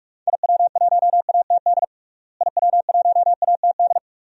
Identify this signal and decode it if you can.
Morse code, heard as a keyed tone, about 46 wpm, 700 Hz. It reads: IW1ATD IW1ATD
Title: DXCC-100 Call Signs - 6 Characters - Repeated 3x - 45wpm